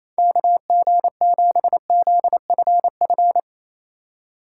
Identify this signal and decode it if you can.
Morse code, heard as a keyed tone, about 37 wpm, 700 Hz. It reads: KG7ZFF